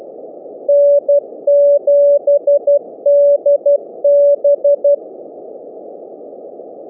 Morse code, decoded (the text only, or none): N7DB